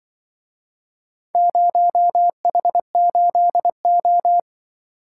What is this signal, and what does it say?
Morse code, heard as a keyed tone, about 24 wpm, 700 Hz. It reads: 0H8O
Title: Speed-Racing - Four Letter-Number Combinations - 20wpm